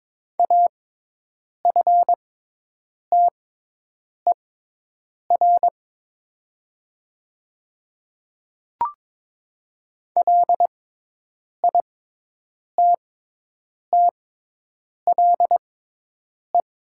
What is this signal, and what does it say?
Morse code, heard as a keyed tone, about 22 wpm, spaced as for 8 wpm, 700 Hz. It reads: AFTER LITTLE
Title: 6x Character Spacing - Top 200 Words - 22wpm